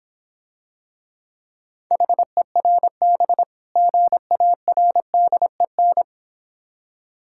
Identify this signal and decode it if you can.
Morse code, heard as a keyed tone, about 26 wpm, 700 Hz. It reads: HERB GARDEN